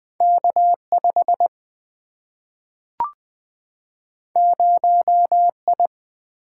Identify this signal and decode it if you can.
Morse code, heard as a keyed tone, about 20 wpm, 700 Hz. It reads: K 5 0 I
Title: Single-Letter Single-Number 20wpm